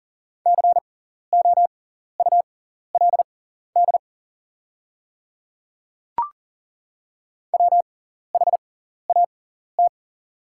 Morse code, heard as a keyed tone, about 40 wpm, spaced as for 14 wpm, 700 Hz. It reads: COULD WHAT
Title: 6x Character Spacing - Top 100 Words - 40wpm